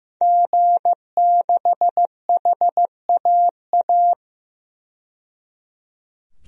Morse code, heard as a keyed tone, about 15 wpm, 700 Hz. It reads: G6HAA
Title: DXCC-100 Call Signs - 5 Characters - 15wpm